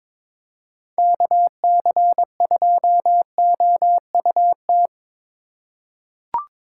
Morse code, heard as a keyed tone, about 22 wpm, 700 Hz. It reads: KC2OUT
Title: USA Call Signs - 6 Characters - 3x Character Spacing - 22wpm